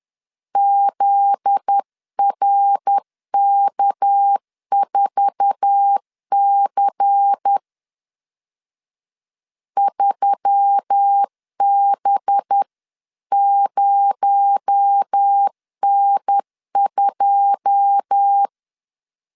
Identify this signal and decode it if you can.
Morse code, heard as a keyed tone, about 10 wpm, 800 Hz. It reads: ZRK4C 3B0N2
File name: LEZIONI / 10 SIGNS / 10A SLASH 10wpm 2808